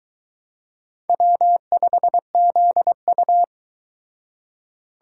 Morse code, heard as a keyed tone, about 21 wpm, 700 Hz. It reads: W5ZU